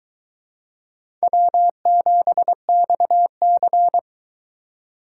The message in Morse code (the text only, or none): W7XC